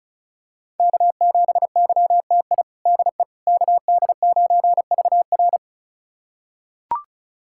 Morse code, heard as a keyed tone, about 36 wpm, 700 Hz. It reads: K7YTI DE KD9VR
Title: USA Callsign DE Callsign - 5 Characters - 35wpm